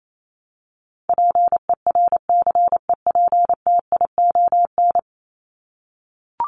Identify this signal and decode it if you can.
Morse code, heard as a keyed tone, about 28 wpm, 700 Hz. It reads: PERCEPTION
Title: Top 1751-2000 Words 28wpm